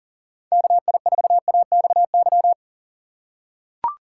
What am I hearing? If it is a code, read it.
Morse code, heard as a keyed tone, about 38 wpm, 700 Hz. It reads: KI4AXY
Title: USA Call Signs - 6 Characters - 3x Character Spacing - 40wpm